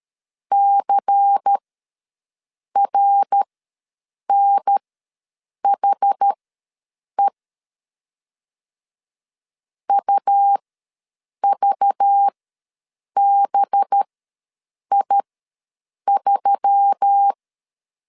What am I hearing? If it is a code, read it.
Morse code, heard as a keyed tone, about 13 wpm, spaced as for 7 wpm, 800 Hz. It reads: CRNHE UVBI3
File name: LEZIONI / 5 LBVC / 5 LBVC ws 12wpm 3159